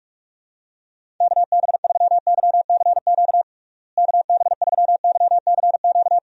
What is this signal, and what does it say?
Morse code, heard as a keyed tone, about 46 wpm, 700 Hz. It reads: KB3YCQ KB3YCQ
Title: USA Call Signs - 6 Characters - Repeated 3x - 45wpm